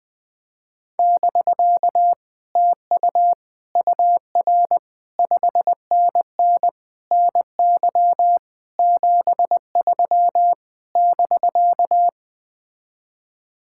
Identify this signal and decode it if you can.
Morse code, heard as a keyed tone, about 20 wpm, 700 Hz. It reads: <BK> TU UR 5NN NY 73 <BK>
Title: Speed-Racing Common QSO Phrases - 17wpm